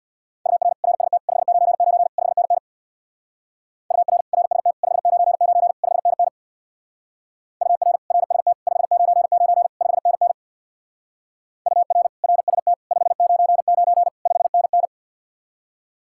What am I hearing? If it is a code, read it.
Morse code, heard as a keyed tone, about 64 wpm, 700 Hz. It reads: UR RST 599 5NN UR RST 599 5NN UR RST 599 5NN UR RST 599 5NN